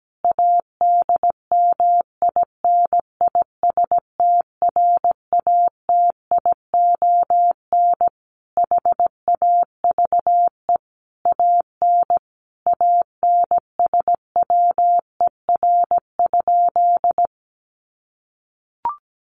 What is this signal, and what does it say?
Morse code, heard as a keyed tone, about 17 wpm, 700 Hz. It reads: ADMINISTRATION HAVE AN ANSWER?
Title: Sentences from Top 1300 Words 17wpm